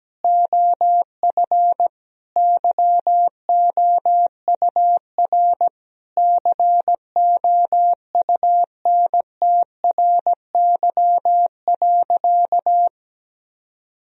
Morse code, heard as a keyed tone, about 17 wpm, 700 Hz. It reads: OF YOUR COUNTRY.
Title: Sentences from Top 1000 Words - Review Part 3 - 17wpm